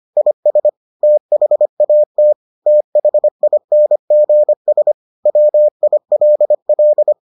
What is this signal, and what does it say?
Morse code, heard as a keyed tone, about 25 wpm, 600 Hz. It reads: IS THAT THINGS WILL